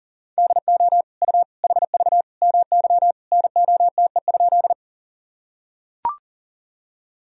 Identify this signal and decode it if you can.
Morse code, heard as a keyed tone, about 40 wpm, 700 Hz. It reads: DO U HV MY NOTE?
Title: Sentences from Top 500 Words and Abbreviations 40wpm